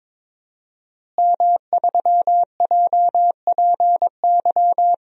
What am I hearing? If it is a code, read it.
Morse code, heard as a keyed tone, about 21 wpm, 700 Hz. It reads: M3JPY
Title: DXCC-100 Call Signs - 5 Characters - Repeated 3x - 22wpm